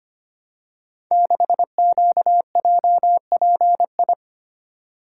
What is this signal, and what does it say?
Morse code, heard as a keyed tone, about 25 wpm, 700 Hz. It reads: 6QJPI
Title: Code Groups - 5 Letter:Numbers - 25wpm